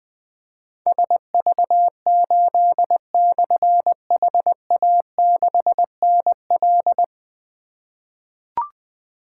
Morse code, heard as a keyed tone, about 20 wpm, 700 Hz. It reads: SV8/HA6NL
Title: Difficult Call Signs 20wpm